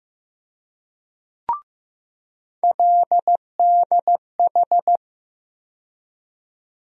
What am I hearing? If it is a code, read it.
Morse code, heard as a keyed tone, about 15 wpm, 700 Hz. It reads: LDH
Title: USA Call Sign Postfixes - 3 Letters 15wpm